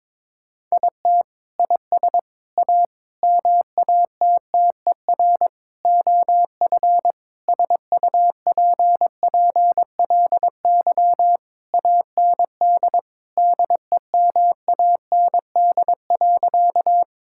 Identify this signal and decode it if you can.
Morse code, heard as a keyed tone, about 21 wpm, 700 Hz. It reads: IT IS A MATTER OF SUPPLY AND DEMAND.
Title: Sentences from Top 1000 Words - Review Part 5 - 22wpm